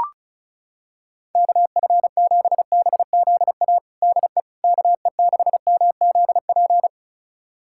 Morse code, heard as a keyed tone, about 36 wpm, 700 Hz. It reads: KF7BZA DE KE6MZP
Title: USA Callsign DE Callsign - 6 Characters - 35wpm